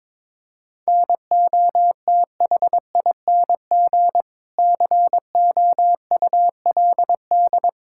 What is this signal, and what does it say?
Morse code, heard as a keyed tone, about 22 wpm, 700 Hz. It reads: NOTHING COULD
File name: practice / 22wpm / Sets of 6 Words - 22wpm